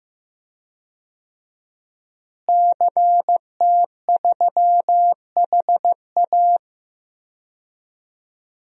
Morse code, heard as a keyed tone, about 15 wpm, 700 Hz. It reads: CT3HA
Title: DXCC-100 Call Signs - 5 Characters - Repeated 2x - 15wpm